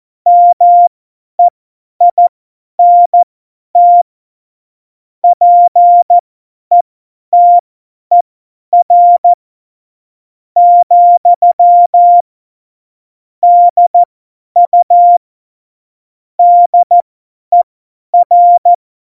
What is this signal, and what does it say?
Morse code, heard as a keyed tone, about 14 wpm, spaced as for 9 wpm, 700 Hz. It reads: MEINT PETER, DU DER
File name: Morsekurs / Witze / Witz 27. Februar 2018, 50 F70